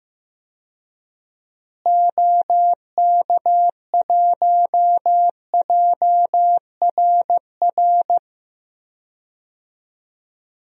Morse code, heard as a keyed tone, about 15 wpm, 700 Hz. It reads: OK1JRR
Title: DXCC-100 Call Signs - 6 Characters - Repeated 3x - 15wpm